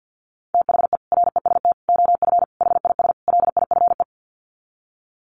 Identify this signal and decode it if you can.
Morse code, heard as a keyed tone, about 48 wpm, 700 Hz. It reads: THE REST OF HIS LIFE
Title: Sets of 5 Words 50wpm